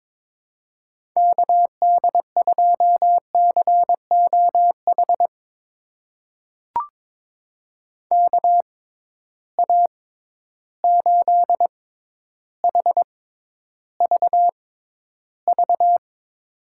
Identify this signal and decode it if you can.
Morse code, heard as a keyed tone, about 22 wpm, spaced as for 11 wpm, 700 Hz. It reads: KD2COH KA8HVV
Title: USA Call Signs - 6 Characters - 6x Character Spacing - 22wpm